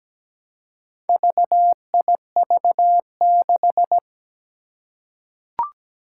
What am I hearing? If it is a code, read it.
Morse code, heard as a keyed tone, about 17 wpm, 700 Hz. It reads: VIV6
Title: Four Challenging Letter-Number Combinations 17wpm